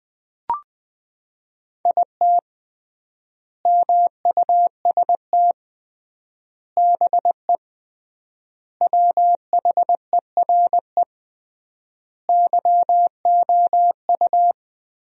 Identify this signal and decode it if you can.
Morse code, heard as a keyed tone, about 21 wpm, 700 Hz. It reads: IT MUST BE WHERE YOU
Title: 3x Word-Spacing - Sentences from Top 200 Words - 20wpm